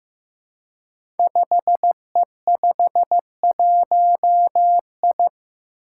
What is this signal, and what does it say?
Morse code, heard as a keyed tone, about 15 wpm, 700 Hz. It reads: 5E51I
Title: Five Challenging Letter-Number Combinations 15wpm